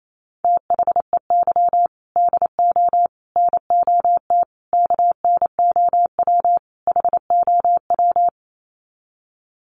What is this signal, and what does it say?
Morse code, heard as a keyed tone, about 28 wpm, 700 Hz. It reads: THEY DO NOT KNOW HOW
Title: Sets of 5 Words Encore2 28wpm